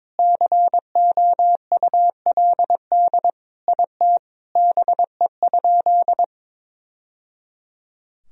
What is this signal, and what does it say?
Morse code, heard as a keyed tone, about 22 wpm, 700 Hz. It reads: COULD IT BE?
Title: Sentences from Top 1000 Words - Review Part 2 - 22wpm